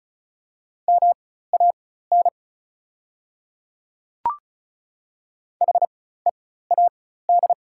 Morse code, heard as a keyed tone, about 36 wpm, spaced as for 16 wpm, 690 Hz. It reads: MAN HEAD
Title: 4x Character Spacing - Top 300 Words - 35wpm